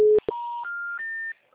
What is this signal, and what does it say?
Special information tone.